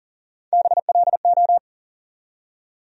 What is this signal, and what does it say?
Morse code, heard as a keyed tone, about 40 wpm, 700 Hz. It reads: BLO
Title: USA Call Sign Postfixes - 3 Letters 40wpm